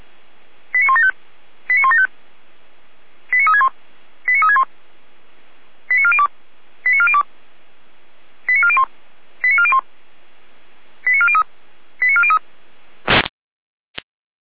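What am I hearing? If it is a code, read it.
Five-tone selective call (ZVEI), heard as quick runs of five tones, tones of about 70 ms. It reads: 89175 89175 89361 89361 89402 89402 89401 89401 89403 89403
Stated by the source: ZVEI five-tone selective call, received on SDR.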